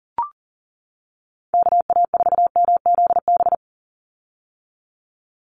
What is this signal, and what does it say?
Morse code, heard as a keyed tone, about 40 wpm, 700 Hz. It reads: KA4MZB
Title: USA Call Signs 40wpm